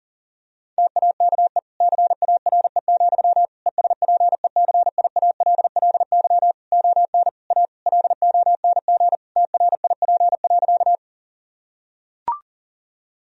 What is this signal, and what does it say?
Morse code, heard as a keyed tone, about 40 wpm, 700 Hz. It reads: TAKE CARE, ESPECIALLY ON A LONG TRIP.
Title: Sentences from Top 1000 Words - Review Part 4 - 40wpm